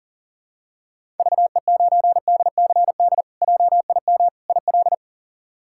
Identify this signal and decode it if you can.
Morse code, heard as a keyed tone, about 37 wpm, 700 Hz. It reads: VE9DCD JIM IL
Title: K1USN SST QSO Practice - 40wpm